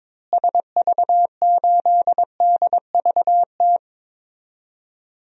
Morse code, heard as a keyed tone, about 22 wpm, 700 Hz. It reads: SV8DVT
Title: DXCC-100 Call Signs - 6 Characters - Repeated 3x - 22wpm